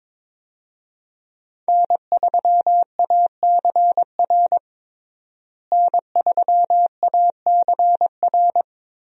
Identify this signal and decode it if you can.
Morse code, heard as a keyed tone, about 21 wpm, 700 Hz. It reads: N3ACR N3ACR
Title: USA Call Signs - 5 Characters - Repeated 3x - 22wpm